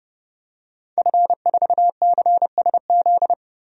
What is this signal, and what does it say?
Morse code, heard as a keyed tone, about 31 wpm, 700 Hz. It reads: F4CSZ
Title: DXCC-100 Call Signs - 5 Characters - 30wpm